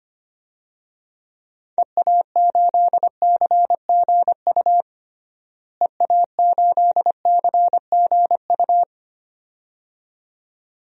Morse code, heard as a keyed tone, about 25 wpm, 700 Hz. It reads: EA8CGU EA8CGU
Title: DXCC-100 Call Signs - 6 Characters - Repeated 2x - 25wpm